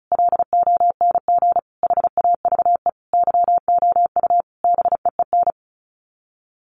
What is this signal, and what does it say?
Morse code, heard as a keyed tone, about 34 wpm, 700 Hz. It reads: LONG HAVE YOU BEEN
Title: Sets of 5 Words Encore 35wpm